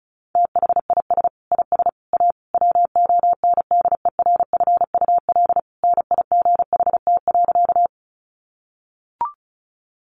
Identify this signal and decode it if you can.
Morse code, heard as a keyed tone, about 34 wpm, 700 Hz. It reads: THIS IS A WONDERFUL NIGHT.
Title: Sentences from Top 1100 Words 35wpm